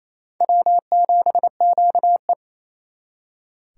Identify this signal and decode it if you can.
Morse code, heard as a keyed tone, about 35 wpm, 700 Hz. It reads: W7QE
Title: Speed Racing - USA 1x2 Call Signs 28wpm